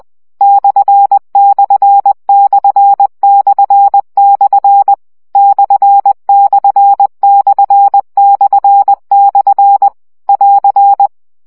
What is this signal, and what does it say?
Morse code, heard as a keyed tone, about 20 wpm, 800 Hz. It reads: ///// ///// <AR>
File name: CW gyakorlas per 17 baud